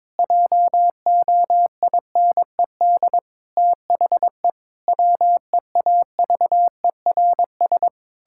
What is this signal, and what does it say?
Morse code, heard as a keyed tone, about 22 wpm, 700 Hz. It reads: JOINED THE WEAVERS
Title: Pangram Challenge v2 - 22wpm